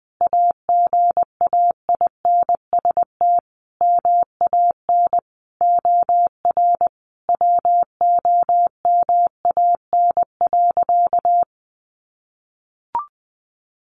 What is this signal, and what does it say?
Morse code, heard as a keyed tone, about 20 wpm, 700 Hz. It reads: AGAINST MAN OR WOMAN.
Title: Sentences from Top 1400 Words 20wpm